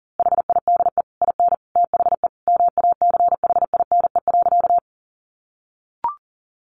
Morse code, heard as a keyed tone, about 37 wpm, 700 Hz. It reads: HIDE IN THE MACHINE.
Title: A Sound Of Thunder 40wpm